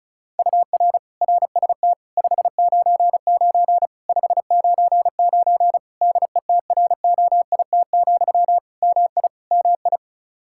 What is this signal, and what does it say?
Morse code, heard as a keyed tone, about 34 wpm, 700 Hz. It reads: UR RST 599 599 DETROIT, MI MI
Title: Practice QSO - Exchange 1 - Beginning - 35wpm